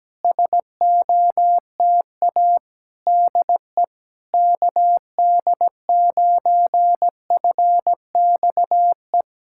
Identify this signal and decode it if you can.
Morse code, heard as a keyed tone, about 17 wpm, 700 Hz. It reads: SOTA DE KD9FXE
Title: SOTA QSO Practice 17wpm